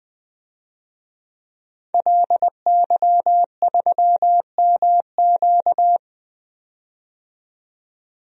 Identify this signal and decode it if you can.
Morse code, heard as a keyed tone, about 20 wpm, 700 Hz. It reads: LY3MQ
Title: DXCC-100 Call Signs - 5 Characters - 20wpm